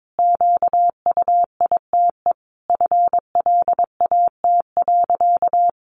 Morse code, heard as a keyed tone, about 22 wpm, 700 Hz. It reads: QUITE FLAT.
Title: Sentences from Top 1000 Words 22wpm